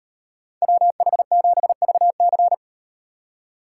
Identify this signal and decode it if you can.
Morse code, heard as a keyed tone, about 38 wpm, 700 Hz. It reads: WH7VC